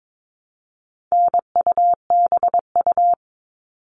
Morse code, heard as a keyed tone, about 22 wpm, 700 Hz. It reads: NUBU
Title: All Cut Numbers - 4-Digit 22wpm